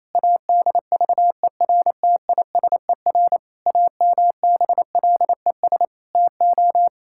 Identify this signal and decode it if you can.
Morse code, heard as a keyed tone, about 28 wpm, 700 Hz. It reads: ADVERTISER AMBLES TO
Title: Pangram Challenge v2 - 28wpm